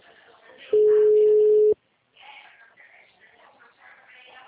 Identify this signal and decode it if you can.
Ringback tone (Europe, 425 Hz), with speech.
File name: out- -2001- - - .6296